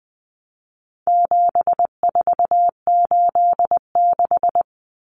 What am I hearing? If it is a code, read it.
Morse code, heard as a keyed tone, about 20 wpm, 700 Hz. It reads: 7486
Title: Common Cut Numbers - 4-Digit 20wpm